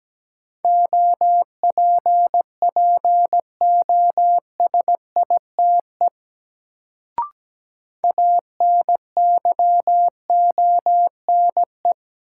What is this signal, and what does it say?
Morse code, heard as a keyed tone, about 17 wpm, 700 Hz. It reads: OPPOSITE ANYONE
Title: Sets of 5 Words Encore - New Word Drill - 17wpm